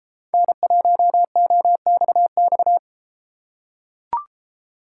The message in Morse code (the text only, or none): N1OXX